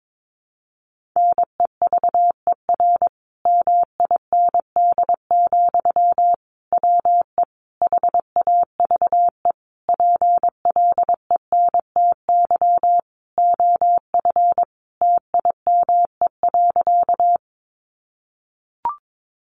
Morse code, heard as a keyed tone, about 22 wpm, 700 Hz. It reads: NEVER MIND, WE HAVE PLENTY OF TIME.
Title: Sentences from Top 1300 Words 22wpm